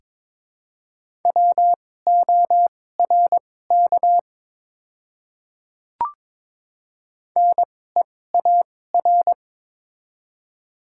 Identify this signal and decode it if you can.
Morse code, heard as a keyed tone, about 22 wpm, spaced as for 16 wpm, 690 Hz. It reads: WORK NEAR
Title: 2x Character Spacing - Top 200 Words - 22wpm